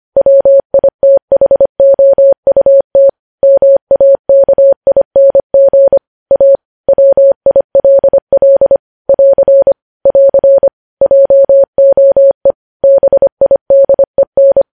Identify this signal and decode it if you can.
Morse code, heard as a keyed tone, about 25 wpm, 550 Hz. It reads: WITHOUT MAKING A WILL <AR> <AR> JOE BIDEN
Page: news headlines in morse code at 25 wpm